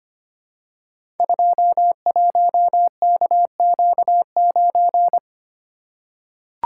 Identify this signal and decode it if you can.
Morse code, heard as a keyed tone, about 25 wpm, 700 Hz. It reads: 21KQ9
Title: Code Groups - 5 Letter:Numbers - 25wpm